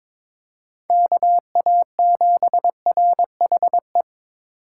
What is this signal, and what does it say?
Morse code, heard as a keyed tone, about 26 wpm, 700 Hz. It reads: KA7RHE
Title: Speed Racing - USA 2x3 Call Signs 22wpm